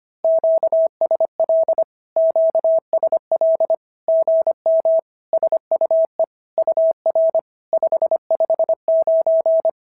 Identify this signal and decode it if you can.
Morse code, heard as a keyed tone, about 25 wpm, 650 Hz. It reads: QSL QSL GM SUE UR 559
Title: SOTA QSO Practice 25wpm